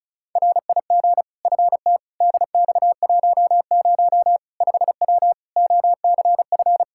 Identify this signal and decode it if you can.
Morse code, heard as a keyed tone, about 36 wpm, 700 Hz. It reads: RIG FT DX10 5W OCF
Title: Practice QSO - Exchange 2 - Info - 35wpm